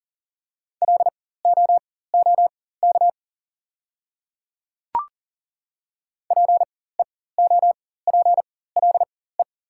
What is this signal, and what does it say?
Morse code, heard as a keyed tone, about 40 wpm, spaced as for 19 wpm, 690 Hz. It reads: LOOK PEOPLE
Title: 4x Character Spacing - Top 100 Words - 40wpm